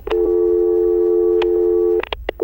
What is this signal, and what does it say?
Dial tone (North America), 350 + 440 Hz.
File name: audio-library / sfx / steady-electronic-clear-crisp-highpitched-1inylew